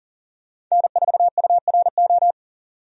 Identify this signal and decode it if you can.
Morse code, heard as a keyed tone, about 40 wpm, 700 Hz. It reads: N4URO
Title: USA Call Signs - 5 Characters - 40wpm